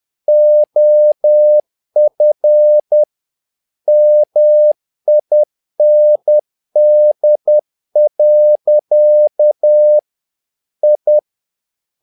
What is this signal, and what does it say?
Morse code, heard as a keyed tone, about 10 wpm, 600 Hz. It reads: OF MIND. I
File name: learning center / tower trouble / tower trouble 10wpm